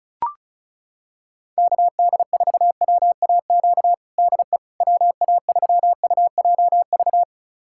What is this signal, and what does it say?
Morse code, heard as a keyed tone, about 34 wpm, 700 Hz. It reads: KD4WAQ DE WA3UJV
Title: USA Callsign DE Callsign 35wpm